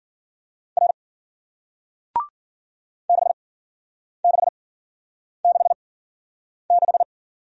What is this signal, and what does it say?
Morse code, heard as a keyed tone, about 54 wpm, spaced as for 9 wpm, 700 Hz. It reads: R 6666